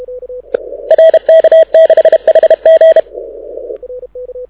Morse code, heard as a keyed tone, about 33 wpm, 640 Hz. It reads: RK6HG
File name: audio zawody / IARU 2015 CW SN7Q